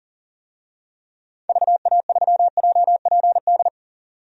Morse code, heard as a keyed tone, about 40 wpm, 700 Hz. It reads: VA3JPD
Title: DXCC-100 Call Signs - 6 Characters - 40wpm